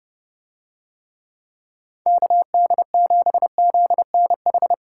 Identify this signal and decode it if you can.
Morse code, heard as a keyed tone, about 30 wpm, 700 Hz. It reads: KD7ZNH